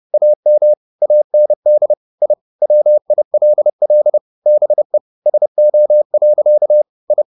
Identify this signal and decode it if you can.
Morse code, heard as a keyed tone, about 30 wpm, 600 Hz. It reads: AM AND I WILL BE SO. I
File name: the old man and the sea / the old man and the sea 1 30wpm